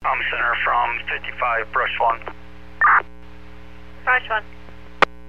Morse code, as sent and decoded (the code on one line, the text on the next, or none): none